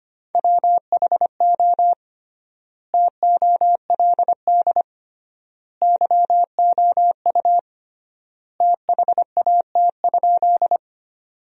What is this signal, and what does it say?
Morse code, heard as a keyed tone, about 25 wpm, 700 Hz. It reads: WHO TOLD YOU THAT?
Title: 3x Word-Spacing - Sentences from Top 400 Words - 25wpm